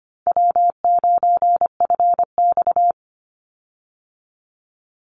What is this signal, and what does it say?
Morse code, heard as a keyed tone, about 25 wpm, 700 Hz. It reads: W9FX
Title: Easy Call Signs Repeated 2x - 25wpm